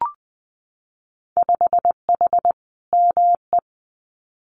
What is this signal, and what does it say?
Morse code, heard as a keyed tone, about 20 wpm, 700 Hz. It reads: 5H ME
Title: Winter Field Day Exchanges Encore 20wpm